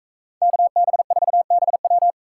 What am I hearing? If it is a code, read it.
Morse code, heard as a keyed tone, about 44 wpm, 700 Hz. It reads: KB4BW